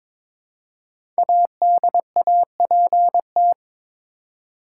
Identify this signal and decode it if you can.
Morse code, heard as a keyed tone, about 22 wpm, 700 Hz. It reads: ADAPT
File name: 5 Letter Words - First 3 Letters Given - 22wpm